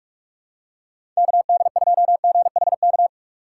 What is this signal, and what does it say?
Morse code, heard as a keyed tone, about 46 wpm, 700 Hz. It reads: KD2GHK
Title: USA Call Signs - 4x Character Spacing - 45wpm